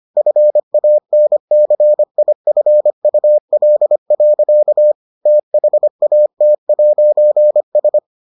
Morse code, heard as a keyed tone, about 25 wpm, 600 Hz. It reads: FANCIFUL. THAT'S